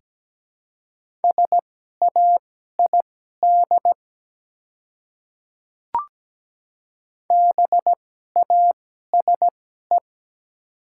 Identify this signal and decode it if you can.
Morse code, heard as a keyed tone, about 17 wpm, spaced as for 12 wpm, 700 Hz. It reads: SAID BASE
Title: 2x Character Spacing - Top 300 Words - 17wpm